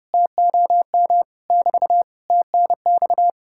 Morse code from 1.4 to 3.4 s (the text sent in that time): <BT> TNX